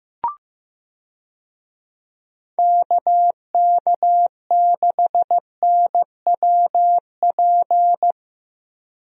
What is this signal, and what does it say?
Morse code, heard as a keyed tone, about 15 wpm, 700 Hz. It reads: KK6NWP
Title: USA Call Signs - 6 Characters - Repeated 2x - 15wpm